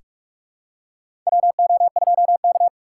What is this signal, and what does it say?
Morse code, heard as a keyed tone, about 46 wpm, 700 Hz. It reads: WO2K
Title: USA Call Signs - 4x Character Spacing - 45wpm